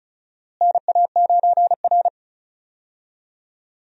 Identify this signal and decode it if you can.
Morse code, heard as a keyed tone, about 36 wpm, 700 Hz. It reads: NA9R
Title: USA Call Signs - 4 Characters - Repeated 2x - 35wpm